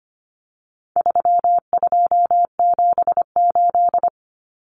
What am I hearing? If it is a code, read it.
Morse code, heard as a keyed tone, about 25 wpm, 700 Hz. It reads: 3278
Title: Common Cut Numbers - 4-Digit 25wpm